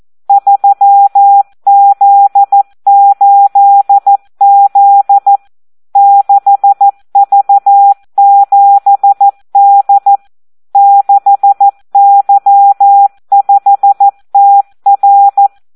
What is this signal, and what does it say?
Morse code, heard as a keyed tone, about 14 wpm, 800 Hz. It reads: 3Z8Z 6V7D 6Y5TR
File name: Audio practica CW / SD 1 14